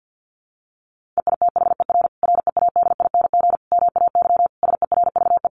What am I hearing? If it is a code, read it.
Morse code, heard as a keyed tone, about 50 wpm, 700 Hz. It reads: EITHER READING MAY SERVE
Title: Sentences from Top 1400 Words 50wpm